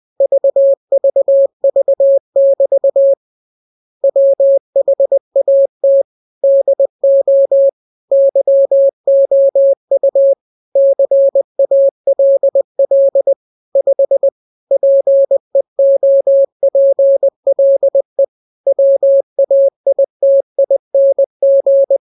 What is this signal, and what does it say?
Morse code, heard as a keyed tone, about 20 wpm, 550 Hz. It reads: VVV<BT> WHAT DO YOU CALL 5 PEOPLE WAITING
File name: jokes / joke 25 20 0000